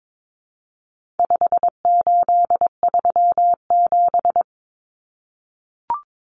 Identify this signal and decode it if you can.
Morse code, heard as a keyed tone, about 22 wpm, 700 Hz. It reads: 5837
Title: Numbers - 4-Digit 22wpm